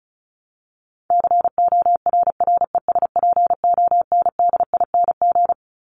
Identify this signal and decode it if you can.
Morse code, heard as a keyed tone, about 34 wpm, 700 Hz. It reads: CORRESPONDING
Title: ING Suffix - Top 100 Words with Base Words 35wpm